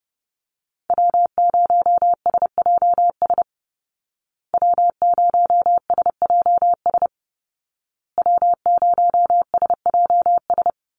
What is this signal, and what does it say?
Morse code, heard as a keyed tone, about 30 wpm, 700 Hz. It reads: W0SJS W0SJS W0SJS
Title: Call Signs Repeated 2x - 30wpm